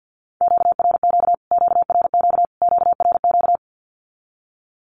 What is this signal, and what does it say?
Morse code, heard as a keyed tone, about 48 wpm, 700 Hz. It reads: QRQ QRQ QRQ
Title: Q-Codes Repeated 3x 50wpm